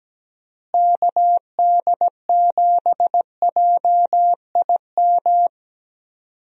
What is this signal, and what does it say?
Morse code, heard as a keyed tone, about 17 wpm, 700 Hz. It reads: KD7JIM
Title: USA Call Signs - 6 Characters - Repeated 3x - 17wpm